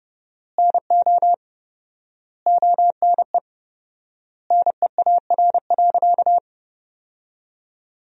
Morse code, heard as a keyed tone, about 32 wpm, 700 Hz. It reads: NO ONE NEAR.
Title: 4x Word-Spacing - Sentences from Top 200 Words - 30wpm